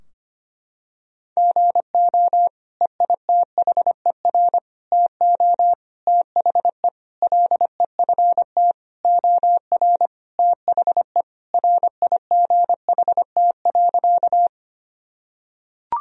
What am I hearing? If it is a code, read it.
Morse code, heard as a keyed tone, about 25 wpm, 700 Hz. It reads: GO EITHER TO THE LEFT OR THE RIGHT.
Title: Sentences from Top 1000 Words - Review Part 4 - 25wpm